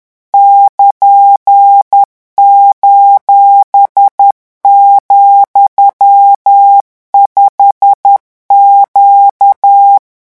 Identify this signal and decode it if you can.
Morse code, heard as a keyed tone, about 10 wpm, 800 Hz. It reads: <KN>:,5Q
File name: morse mp3 / revision16-vitesse 10 mots